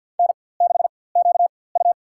Morse code, heard as a keyed tone, about 47 wpm, spaced as for 23 wpm, 700 Hz. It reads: N6QU
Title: USA Call Signs - 4x Character Spacing - 50wpm